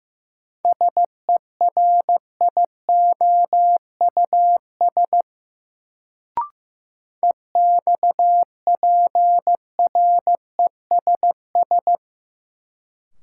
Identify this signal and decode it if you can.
Morse code, heard as a keyed tone, about 15 wpm, 700 Hz. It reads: SERIOUS EXPRESS
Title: Sentences from Top 1100 Words - New Word Drill - 15wpm